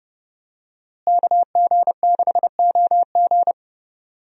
Morse code, heard as a keyed tone, about 32 wpm, 700 Hz. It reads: KG6OG
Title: USA Call Signs - 5 Characters - Repeated 2x - 30wpm